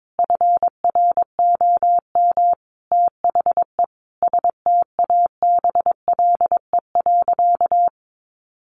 Morse code, heard as a keyed tone, about 22 wpm, 700 Hz. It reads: FROM THE STABLE.
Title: Sentences from Top 1500 Words 22wpm